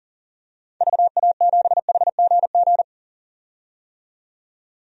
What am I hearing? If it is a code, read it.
Morse code, heard as a keyed tone, about 40 wpm, 700 Hz. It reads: VA7HGG
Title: DXCC-100 Call Signs - 6 Characters - 40wpm